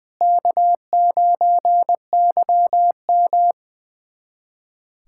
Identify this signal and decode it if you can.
Morse code, heard as a keyed tone, about 20 wpm, 700 Hz. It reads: K9YM
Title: USA Callsign DE Callsign - 4 Characters - 20wpm